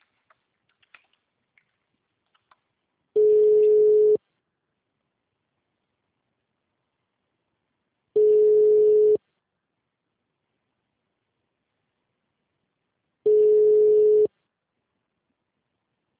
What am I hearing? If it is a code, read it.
Ringback tone (Europe, 425 Hz).